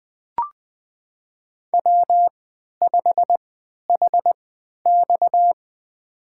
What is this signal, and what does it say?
Morse code, heard as a keyed tone, about 21 wpm, spaced as for 12 wpm, 700 Hz. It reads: W5HX
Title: USA Call Signs - 4 Characters - 3x Character Spacing - 20wpm